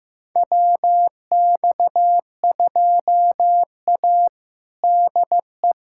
Morse code, heard as a keyed tone, about 15 wpm, 700 Hz. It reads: WX2A DE
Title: USA Callsign DE Callsign - 4 Characters - 15wpm